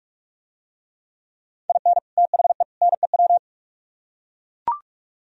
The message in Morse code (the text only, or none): IN THE NEW